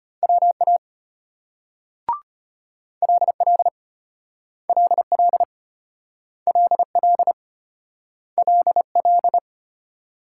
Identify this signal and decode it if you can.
Morse code, heard as a keyed tone, about 35 wpm, 700 Hz. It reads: WA LL LL LL LL